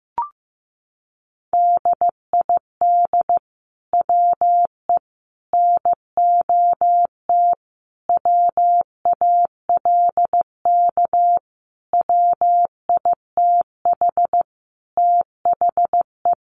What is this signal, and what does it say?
Morse code, heard as a keyed tone, about 15 wpm, 700 Hz. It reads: DID WE NOT WALK WITH THE
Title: Sentences from Top 1400 Words 15wpm